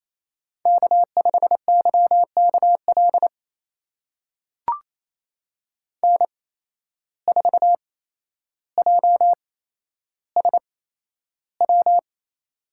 Morse code, heard as a keyed tone, about 28 wpm, spaced as for 12 wpm, 700 Hz. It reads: K5YKL N4JSW
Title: USA Call Signs - 5 Characters - 8x Character Spacing - 28wpm